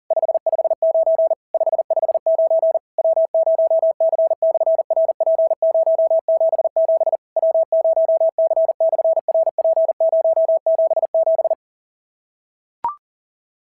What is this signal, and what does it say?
Morse code, heard as a keyed tone, about 37 wpm, 690 Hz. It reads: 559 559 W0C/RP077 W0C/RP077
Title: SOTA QSO Practice 40wpm